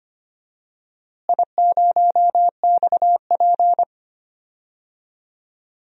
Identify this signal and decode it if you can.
Morse code, heard as a keyed tone, about 25 wpm, 700 Hz. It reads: I0XP
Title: DXCC-100 Call Signs - 4 Characters - Repeated 2x - 25wpm